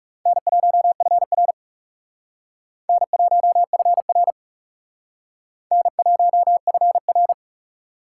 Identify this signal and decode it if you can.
Morse code, heard as a keyed tone, about 36 wpm, 700 Hz. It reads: N1FR N1FR N1FR
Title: Speed Racing - USA 1x2 Call Signs 30wpm